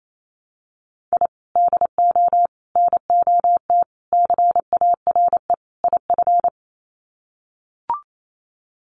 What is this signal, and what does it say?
Morse code, heard as a keyed tone, about 28 wpm, 700 Hz. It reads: I DO NOT CARE IF
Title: Sets of 5 Words 28wpm